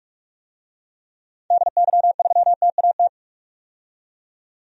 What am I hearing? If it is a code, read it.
Morse code, heard as a keyed tone, about 46 wpm, 700 Hz. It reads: DY3TAT
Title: DXCC-100 Call Signs - 6 Characters - 45wpm